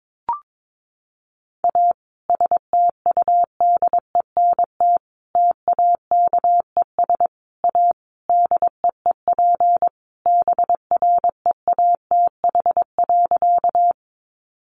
Morse code, heard as a keyed tone, about 22 wpm, 700 Hz. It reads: A STUDENT TAKES A DEEP BREATH.
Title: Additional Easy Phrases 22wpm